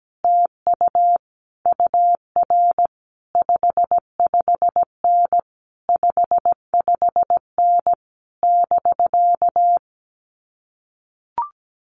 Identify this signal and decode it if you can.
Morse code, heard as a keyed tone, about 17 wpm, 700 Hz. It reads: TU UR 55N 55N <BK>
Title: Common QSO Phrases Encore 17wpm